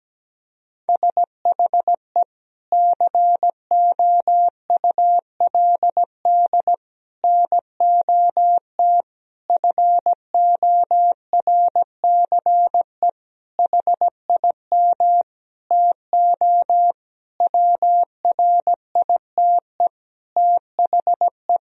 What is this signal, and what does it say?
Morse code, heard as a keyed tone, about 17 wpm, 700 Hz. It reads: SHE COULD NOT FORCE HIM TO WRITE THE
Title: Sentences from Top 1000 Words - Review Part 3 - 17wpm